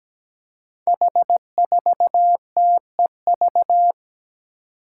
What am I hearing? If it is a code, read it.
Morse code, heard as a keyed tone, about 17 wpm, 700 Hz. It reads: H4TEV
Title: Five Challenging Letter-Number Combinations 17wpm